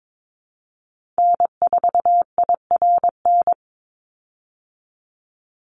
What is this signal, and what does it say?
Morse code, heard as a keyed tone, about 22 wpm, 700 Hz. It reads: N4IRN
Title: Call Signs Repeated 3x - 22wpm